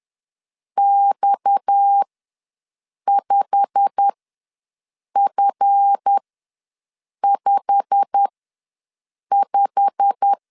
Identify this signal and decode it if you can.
Morse code, heard as a keyed tone, about 10 wpm, spaced as for 6 wpm, 800 Hz. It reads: X5F55
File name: LEZIONI / 8 YJ45 / 8 YJ45 ws 10wpm 3843